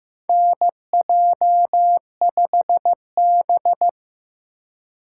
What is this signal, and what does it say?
Morse code, heard as a keyed tone, about 15 wpm, 700 Hz. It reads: NJ5B
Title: USA Call Signs - 4 Characters - 4x Character Spacing - 15wpm